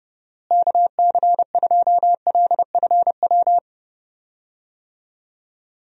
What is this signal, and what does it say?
Morse code, heard as a keyed tone, about 30 wpm, 700 Hz. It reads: KC2LFW
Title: USA Call Signs - 6 Characters - Repeated 2x - 30wpm